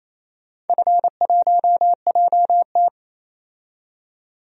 Morse code, heard as a keyed tone, about 28 wpm, 700 Hz. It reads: F1JT
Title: DXCC-100 Call Signs - 4 Characters - 28wpm